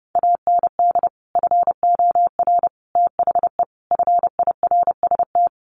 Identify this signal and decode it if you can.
Morse code, heard as a keyed tone, about 30 wpm, 700 Hz. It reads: AND FOR THE FIRST
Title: Sets of 5 Words 30wpm